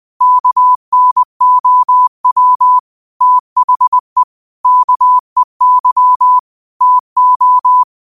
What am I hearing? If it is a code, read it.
Morse code, heard as a keyed tone, about 20 wpm, 1000 Hz. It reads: KNOW THE KEY TO
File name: qotdfiles / 20wpm